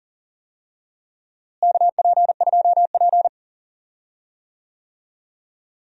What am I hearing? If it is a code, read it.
Morse code, heard as a keyed tone, about 40 wpm, 700 Hz. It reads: KP2P
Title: DXCC-100 Call Signs - 4 Characters - Repeated 2x - 40wpm